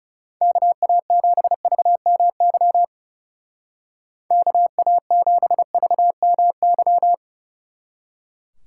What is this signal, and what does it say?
Morse code, heard as a keyed tone, about 39 wpm, 700 Hz. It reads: KA7VMY KA7VMY